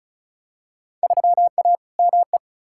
Morse code, heard as a keyed tone, about 34 wpm, 700 Hz. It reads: 3A ME
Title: Field Day Exchanges - 35wpm